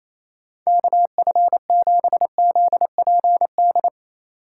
Morse code, heard as a keyed tone, about 28 wpm, 700 Hz. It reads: KF7ZPD
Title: USA Call Signs - 6 Characters - 3x Character Spacing - 28wpm